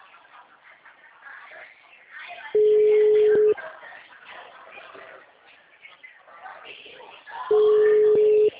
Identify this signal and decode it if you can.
Ringback tone (Europe, 425 Hz), with speech.